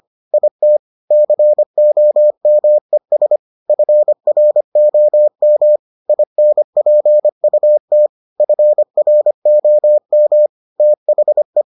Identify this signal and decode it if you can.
Morse code, heard as a keyed tone, about 25 wpm, 600 Hz. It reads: IT COMES FROM INPUT FROM THE